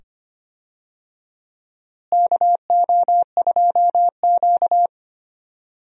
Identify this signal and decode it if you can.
Morse code, heard as a keyed tone, about 25 wpm, 700 Hz. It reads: KO2Q
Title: USA Call Signs - 4 Characters - Repeated 2x - 25wpm